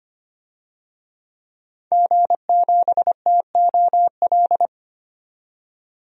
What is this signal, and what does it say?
Morse code, heard as a keyed tone, about 25 wpm, 700 Hz. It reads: G7TOL
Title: DXCC-100 Call Signs - 5 Characters - 25wpm